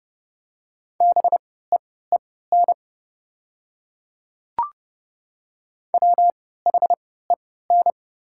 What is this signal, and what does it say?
Morse code, heard as a keyed tone, about 30 wpm, spaced as for 17 wpm, 700 Hz. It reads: BEEN WHEN
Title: 3x Character Spacing - Top 300 Words - 30wpm